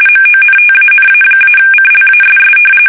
RTTY, about 45.45 baud. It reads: MONTANA-CLASS
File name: fsk850 / battleshipnj